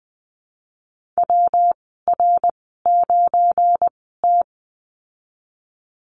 Morse code, heard as a keyed tone, about 20 wpm, spaced as for 14 wpm, 700 Hz. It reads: WR9T
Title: 213 Lesson 34g - Number 8 - Call Signs 20wpm